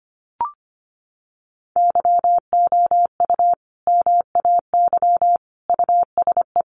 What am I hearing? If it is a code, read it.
Morse code, heard as a keyed tone, about 25 wpm, 700 Hz. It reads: YOU MAY USE
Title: Sentences from Top 1400 Words 25wpm